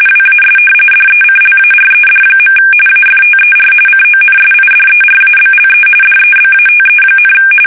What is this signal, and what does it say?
RTTY, about 45.45 baud. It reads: WAS DECLARED. AFTER CONSIDERABLE DEBATE,
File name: fsk850 / usnavy1